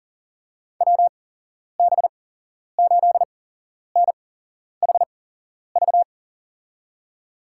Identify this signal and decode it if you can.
Morse code, heard as a keyed tone, about 38 wpm, spaced as for 11 wpm, 700 Hz. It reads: WB8NHV
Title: USA Call Signs - 6 Characters - 8x Character Spacing - 40wpm